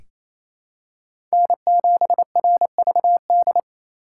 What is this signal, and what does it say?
Morse code, heard as a keyed tone, about 28 wpm, 700 Hz. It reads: N7RVD
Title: K1USN SST QSO Practice - 28wpm